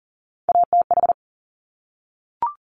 Morse code, heard as a keyed tone, about 40 wpm, 700 Hz. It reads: ATH
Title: Top Three Letter Combinations in Words 40wpm